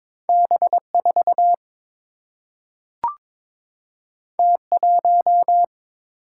Morse code, heard as a keyed tone, about 22 wpm, 700 Hz. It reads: B4 T1
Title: Single-Letter Single-Number 22wpm